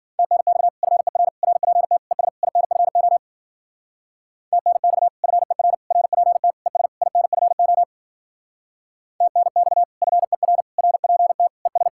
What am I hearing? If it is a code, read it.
Morse code, heard as a keyed tone, about 50 wpm, 700 Hz. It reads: TNX FER RPT ES INFO TNX FER RPT ES INFO TNX FER RPT ES